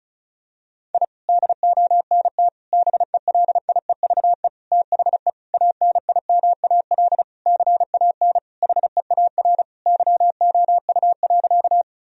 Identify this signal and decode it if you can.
Morse code, heard as a keyed tone, about 34 wpm, 700 Hz. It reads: I DONT BELIEVE THE ANIMAL CAN HEAR YOU.
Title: Sentences from Top 1000 Words - Review Part 1 - 35wpm